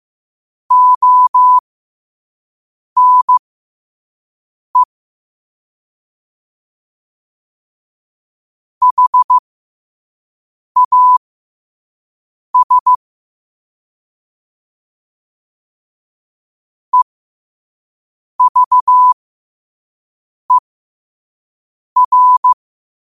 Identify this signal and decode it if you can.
Morse code, heard as a keyed tone, about 15 wpm, spaced as for 5 wpm, 1000 Hz. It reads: ONE HAS EVER
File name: qotdfiles / 5wpm